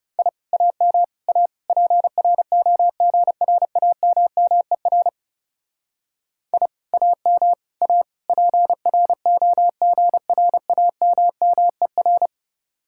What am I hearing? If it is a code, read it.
Morse code, heard as a keyed tone, about 37 wpm, 700 Hz. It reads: I AM A PROGRAMMER I AM A PROGRAMMER
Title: Speed-Racing Common QSO Phrases - 30wpm